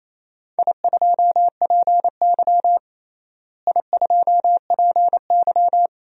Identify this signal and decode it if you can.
Morse code, heard as a keyed tone, about 28 wpm, 700 Hz. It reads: I2PY I2PY
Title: DXCC-100 Call Signs - 4 Characters - Repeated 3x - 28wpm